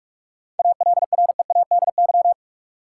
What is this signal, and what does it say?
Morse code, heard as a keyed tone, about 41 wpm, 700 Hz. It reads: ALREADY
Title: Sentences from Top 700 Words - New Word Drill - 45wpm